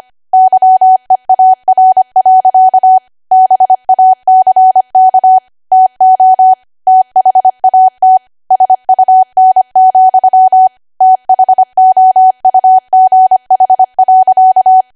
Morse code, heard as a keyed tone, about 25 wpm, 750 Hz. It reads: YEAR. BACK TO THAT SUN, THOUGH.
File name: cwa / W1AW-Level3 / 25WPM